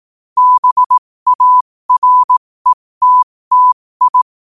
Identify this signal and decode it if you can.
Morse code, heard as a keyed tone, about 18 wpm, spaced as for 15 wpm, 1000 Hz. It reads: BARETTI
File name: qotdfiles / 15wpm